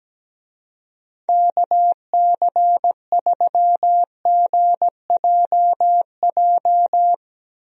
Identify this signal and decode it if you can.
Morse code, heard as a keyed tone, about 17 wpm, 700 Hz. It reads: KC3GJJ
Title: USA Call Signs - 6 Characters - 17wpm